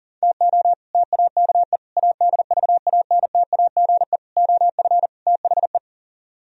Morse code, heard as a keyed tone, about 39 wpm, 700 Hz. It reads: TO TAKE ADVANTAGE OF THE